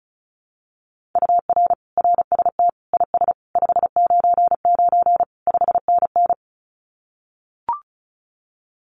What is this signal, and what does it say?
Morse code, heard as a keyed tone, about 34 wpm, 700 Hz. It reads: UR RST IS 599 5NN
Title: Common QSO Phrases Encore 35wpm